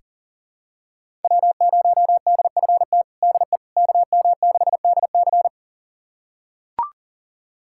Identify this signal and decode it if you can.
Morse code, heard as a keyed tone, about 40 wpm, 700 Hz. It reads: W0DFT DE KM6DC
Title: USA Callsign DE Callsign - 5 Characters - 40wpm